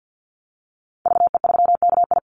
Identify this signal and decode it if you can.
Morse code, heard as a keyed tone, about 50 wpm, 700 Hz. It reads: VE3KI
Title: Call Signs - 50wpm